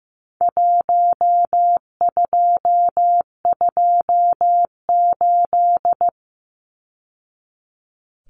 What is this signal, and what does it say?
Morse code, heard as a keyed tone, about 15 wpm, 700 Hz. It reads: 1228
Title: Numbers - Contest Style - 4-Digit 15wpm